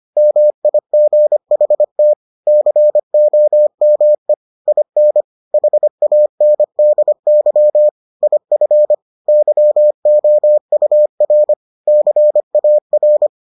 Morse code, heard as a keyed tone, about 25 wpm, 600 Hz. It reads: MIGHT COME IN HANDY IF YOUR CAR